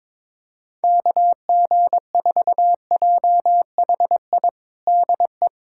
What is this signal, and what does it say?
Morse code, heard as a keyed tone, about 21 wpm, 700 Hz. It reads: KG4JHI DE
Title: USA Callsign DE Callsign 22wpm